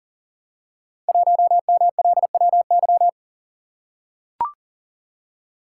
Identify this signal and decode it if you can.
Morse code, heard as a keyed tone, about 43 wpm, 700 Hz. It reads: 1MLWY
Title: Code Groups - 5 Letter:Numbers - 40wpm